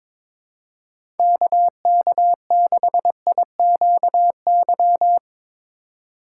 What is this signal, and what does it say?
Morse code, heard as a keyed tone, about 22 wpm, 700 Hz. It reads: KK6IQY
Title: USA Call Signs - 6 Characters - 2x Character Spacing - 22wpm